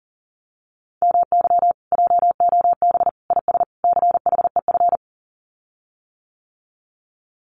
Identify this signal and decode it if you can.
Morse code, heard as a keyed tone, about 43 wpm, 700 Hz. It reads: MY JOB IS CHEF
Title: Common QSO Phrases Encore 40wpm